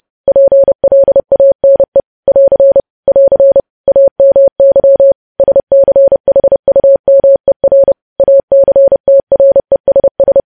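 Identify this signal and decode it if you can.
Morse code, heard as a keyed tone, about 30 wpm, 550 Hz. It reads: PLANE <AR> <AR> AMY SCHUMER ACTRESS
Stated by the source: News headlines sent in Morse at 30 words per minute.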